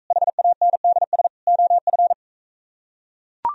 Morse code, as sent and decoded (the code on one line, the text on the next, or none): .... .- -. -.. ... / --- ..-.
HANDS OF